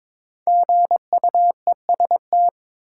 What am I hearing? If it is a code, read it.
Morse code, heard as a keyed tone, about 22 wpm, 700 Hz. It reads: GUEST
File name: 5 Letter Words - First 3 Letters Given - 22wpm